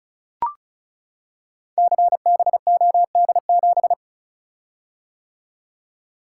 Morse code, heard as a keyed tone, about 36 wpm, 700 Hz. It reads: CBOD7
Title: Code Groups - 5 Letter:Numbers - 35wpm